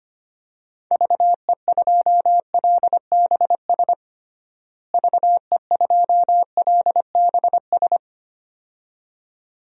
Morse code, heard as a keyed tone, about 25 wpm, 700 Hz. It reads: VE2LBS VE2LBS
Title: DXCC-100 Call Signs - 6 Characters - Repeated 2x - 25wpm